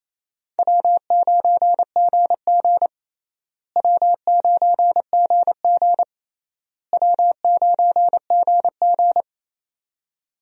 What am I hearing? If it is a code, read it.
Morse code, heard as a keyed tone, about 27 wpm, 700 Hz. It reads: W9GG W9GG W9GG
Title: USA Call Signs - 4 Characters - Repeated 3x - 28wpm